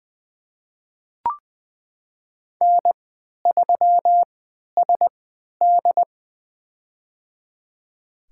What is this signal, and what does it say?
Morse code, heard as a keyed tone, about 20 wpm, spaced as for 11 wpm, 700 Hz. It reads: N3SD
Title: USA Call Signs - 3x Character Spacing - 20wpm